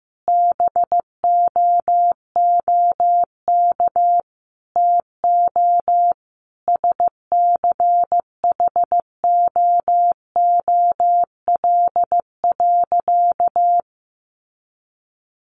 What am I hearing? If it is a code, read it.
Morse code, heard as a keyed tone, about 15 wpm, 700 Hz. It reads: BOOK TO SCHOOL.
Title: Sentences from Top 1400 Words 15wpm